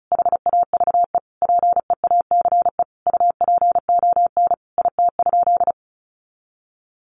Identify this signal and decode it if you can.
Morse code, heard as a keyed tone, about 34 wpm, 700 Hz. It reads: HAVE PEACE UPON IT?
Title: Sentences from Top 1500 Words 35wpm